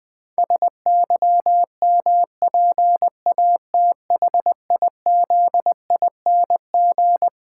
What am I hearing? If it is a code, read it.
Morse code, heard as a keyed tone, about 20 wpm, 700 Hz. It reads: SYMPATHIZING
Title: Pangram Challenge v2 - 20wpm